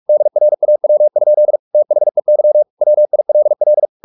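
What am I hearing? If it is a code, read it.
Morse code, heard as a keyed tone, about 44 wpm, 600 Hz. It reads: DRAW? THEY WILL
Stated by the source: Morse practice at 45 wpm, the novel Never by Ken Follett in English, from CW med Gnister.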